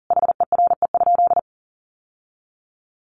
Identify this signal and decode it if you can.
Morse code, heard as a keyed tone, about 40 wpm, 700 Hz. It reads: HERE?
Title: Sentences from Top 1100 Words 40wpm